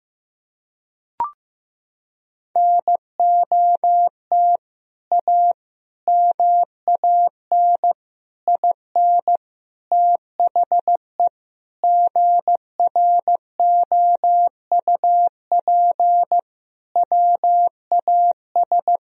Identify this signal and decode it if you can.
Morse code, heard as a keyed tone, about 15 wpm, 700 Hz. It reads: NOT A MAN IN THE GROUP WAS
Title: Sentences from Top 1000 Words - Review Part 1 - 15wpm